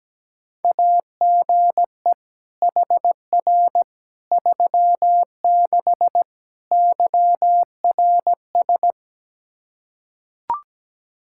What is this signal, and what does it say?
Morse code, heard as a keyed tone, about 17 wpm, 700 Hz. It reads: AGE HR 36 YRS
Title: Practice QSO - Exchange 3 - More Info - 17wpm